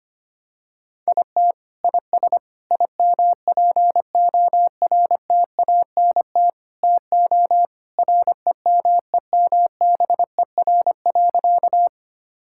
Morse code, heard as a keyed tone, about 25 wpm, 700 Hz. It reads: IT IS IMPORTANT TO REMEMBER.
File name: practice / 25wpm / Sentences with 4-Word Phrases - 25wpm